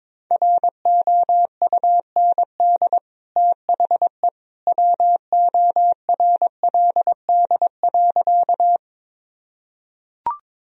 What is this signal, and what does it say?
Morse code, heard as a keyed tone, about 22 wpm, 700 Hz. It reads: ROUND THE WORLD.
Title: Sentences from Top 1000 Words - Review Part 1 - 22wpm